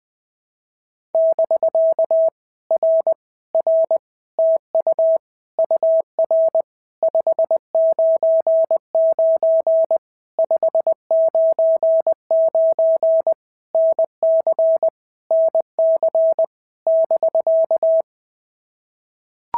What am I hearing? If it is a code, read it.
Morse code, heard as a keyed tone, about 20 wpm, 650 Hz. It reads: <BK> R R TU UR 599 599 NC NC <BK>
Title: POTA QSO Practice - 20wpm